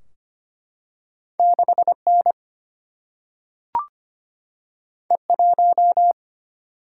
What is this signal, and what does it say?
Morse code, heard as a keyed tone, about 25 wpm, 700 Hz. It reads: 6N E1
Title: Single-Letter Single-Number 25wpm